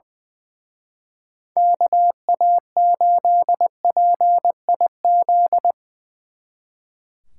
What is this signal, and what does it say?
Morse code, heard as a keyed tone, about 23 wpm, 700 Hz. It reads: KA8PIZ
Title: Speed Racing - USA 2x3 Call Signs 20wpm